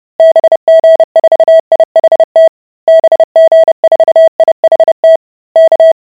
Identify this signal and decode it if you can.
Morse code, heard as a keyed tone, about 30 wpm, 650 Hz. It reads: BG4IHT BG4IHT K